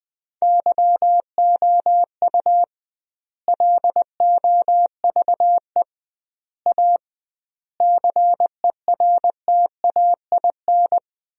Morse code, heard as a keyed tone, about 20 wpm, 700 Hz. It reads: YOU LOVE A CERTAIN
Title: 2x Word-Spacing - Sentences from Top 500 Words - 20wpm